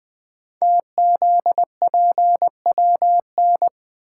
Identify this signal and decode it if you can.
Morse code, heard as a keyed tone, about 20 wpm, 700 Hz. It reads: TZPWN
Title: Code Groups - 5 Letter:Numbers - 20wpm